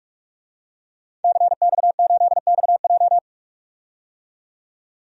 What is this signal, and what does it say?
Morse code, heard as a keyed tone, about 42 wpm, 700 Hz. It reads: CX8XJ
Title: DXCC-100 Call Signs - 5 Characters - 45wpm